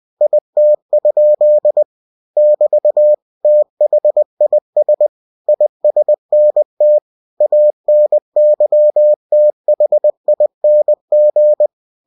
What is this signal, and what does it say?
Morse code, heard as a keyed tone, about 20 wpm, 600 Hz. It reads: IT? <BT> THIS ISNT ANYTHING
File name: cw-books / winnie-the-pooh / winnie-20wpm-page-0105